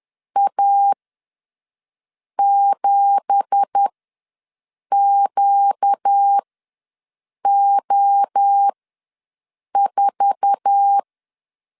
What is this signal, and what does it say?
Morse code, heard as a keyed tone, about 10 wpm, spaced as for 6 wpm, 800 Hz. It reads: A7QO4
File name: LEZIONI / 9 2867 ws 10wpm 4215